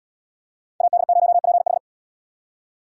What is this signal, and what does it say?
Morse code, heard as a keyed tone, about 70 wpm, 700 Hz. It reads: AD0PH